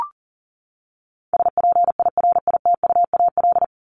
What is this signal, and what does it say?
Morse code, heard as a keyed tone, about 40 wpm, 700 Hz. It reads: SPIRITUAL
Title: Top 1501-1750 Words 40wpm